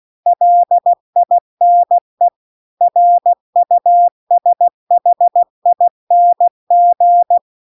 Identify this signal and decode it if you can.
Morse code, heard as a keyed tone, about 16 wpm, 700 Hz. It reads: LINE RUSHING
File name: CW-The-Big-Twoheaded-River-2-16WPM